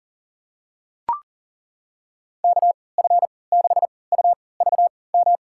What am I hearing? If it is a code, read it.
Morse code, heard as a keyed tone, about 40 wpm, spaced as for 23 wpm, 700 Hz. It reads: KF6UVM
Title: USA Call Signs - 6 Characters - 3x Character Spacing - 40wpm